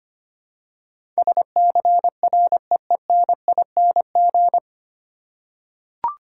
Morse code, heard as a keyed tone, about 25 wpm, 700 Hz. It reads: SCREENING
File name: practice / 25wpm / 9 Letter Words - First 3 Letters Given - 25wpm